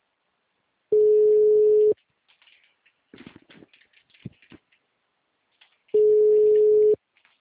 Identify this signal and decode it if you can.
Ringback tone (Europe, 425 Hz).
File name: out- -2023- - - .56